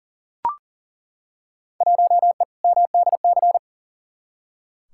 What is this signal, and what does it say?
Morse code, heard as a keyed tone, about 40 wpm, 700 Hz. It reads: 1E MDC
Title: Field Day Exchanges - 40wpm